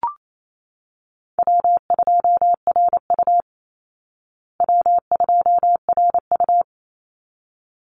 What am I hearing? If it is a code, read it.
Morse code, heard as a keyed tone, about 28 wpm, 700 Hz. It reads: W2RU W2RU
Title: Easy Call Signs Repeated 2x - 28wpm